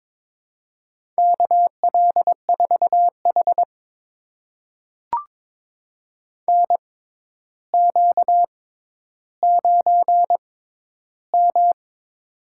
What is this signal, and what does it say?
Morse code, heard as a keyed tone, about 22 wpm, spaced as for 11 wpm, 700 Hz. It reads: KL4H NQ9M
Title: USA Call Signs - 4 Characters - 6x Character Spacing - 22wpm